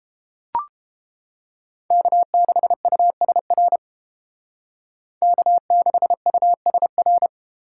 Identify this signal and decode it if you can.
Morse code, heard as a keyed tone, about 31 wpm, 700 Hz. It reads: K6USR K6USR